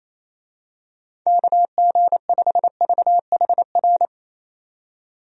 Morse code, heard as a keyed tone, about 28 wpm, 700 Hz. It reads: KG5VHR
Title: USA Call Signs - 6 Characters - 28wpm